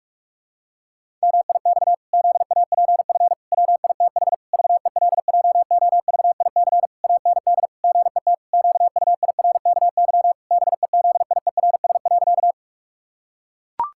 Morse code, heard as a keyed tone, about 43 wpm, 700 Hz. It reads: MIX ZAPF WITH VELJOVIC AND GET QUIRKY BEZIERS.
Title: Pangram Challenge v2 - 45wpm